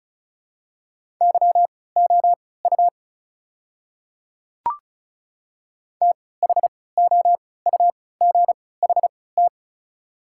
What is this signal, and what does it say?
Morse code, heard as a keyed tone, about 36 wpm, spaced as for 20 wpm, 690 Hz. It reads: YOU THOUGHT
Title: 3x Character Spacing - Top 300 Words - 35wpm